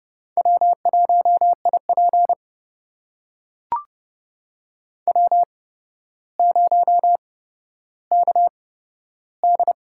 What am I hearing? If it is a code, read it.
Morse code, heard as a keyed tone, about 30 wpm, spaced as for 13 wpm, 700 Hz. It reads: W1IP W0KD
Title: USA Call Signs - 4 Characters - 8x Character Spacing - 30wpm